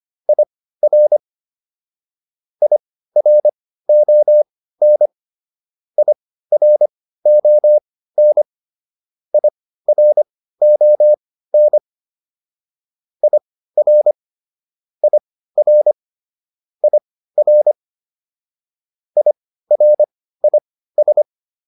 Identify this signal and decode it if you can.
Morse code, heard as a keyed tone, about 25 wpm, spaced as for 15 wpm, 600 Hz. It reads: IR IRON IRON IRON IR IR IR IRIS